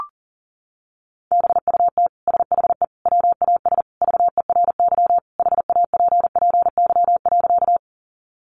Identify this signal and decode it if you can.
Morse code, heard as a keyed tone, about 40 wpm, 700 Hz. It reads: BUT SHE WAS VERY HAPPY.
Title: Sentences from Top 1000 Words 40wpm